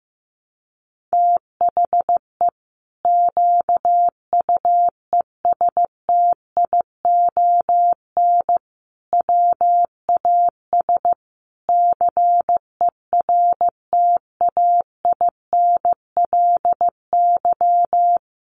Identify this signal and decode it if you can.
Morse code, heard as a keyed tone, about 15 wpm, 700 Hz. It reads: THE QUESTION WAS CERTAINLY
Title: Sentences from Top 1200 Words 15wpm